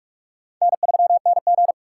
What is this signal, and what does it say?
Morse code, heard as a keyed tone, about 49 wpm, 700 Hz. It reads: N3NG